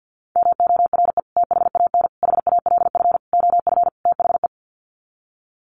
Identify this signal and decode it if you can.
Morse code, heard as a keyed tone, about 50 wpm, 700 Hz. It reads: MORE THAN HALF OF THE
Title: Sets of 5 Words 50wpm